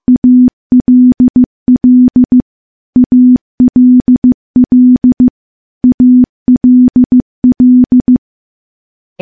Morse code, heard as a keyed tone, about 15 wpm, 250 Hz. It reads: ALL ALL ALL
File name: morse / morse-training-15WPM